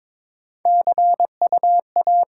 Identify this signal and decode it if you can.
Morse code, heard as a keyed tone, about 22 wpm, 700 Hz. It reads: C U A
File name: Top 20 QSO Elements - 8x Character Spacing - 22wpm